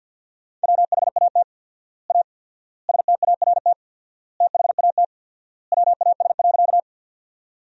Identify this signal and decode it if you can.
Morse code, heard as a keyed tone, about 50 wpm, 700 Hz. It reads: WHAT A START THAT WAS.
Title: 4x Word-Spacing - Sentences from Top 300 Words - 50wpm